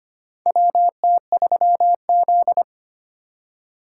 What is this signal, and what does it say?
Morse code, heard as a keyed tone, about 25 wpm, 700 Hz. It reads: WT3Z
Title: USA Call Signs - 4 Characters - 3x Character Spacing - 25wpm